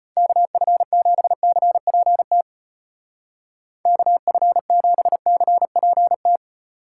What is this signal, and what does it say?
Morse code, heard as a keyed tone, about 30 wpm, 700 Hz. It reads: KF7CPT KF7CPT
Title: Speed Racing - USA Call Signs 25wpm